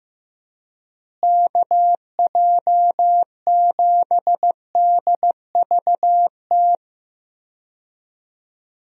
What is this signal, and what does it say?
Morse code, heard as a keyed tone, about 16 wpm, 700 Hz. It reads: KJ7DVT
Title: USA Call Signs - 6 Characters - Repeated 3x - 15wpm